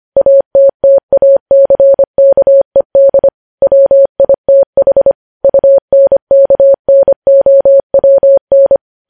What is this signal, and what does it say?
Morse code, heard as a keyed tone, about 25 wpm, 550 Hz. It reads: ATTACKED WITH UNKNOWN
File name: podcast / News headlines-1x-25x25